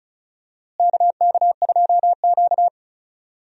Morse code, heard as a keyed tone, about 36 wpm, 700 Hz. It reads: KK2Q
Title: USA Call Signs - 4 Characters - Repeated 3x - 35wpm